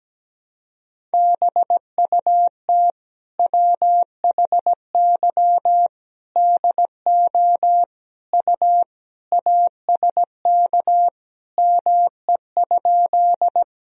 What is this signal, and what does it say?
Morse code, heard as a keyed tone, about 17 wpm, 700 Hz. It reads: BUT WHY DO U ASK ME?
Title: Sentences from Top 200 Words and Abbreviations 17wpm